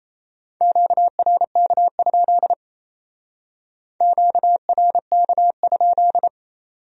Morse code, heard as a keyed tone, about 35 wpm, 700 Hz. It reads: QRK? QRK?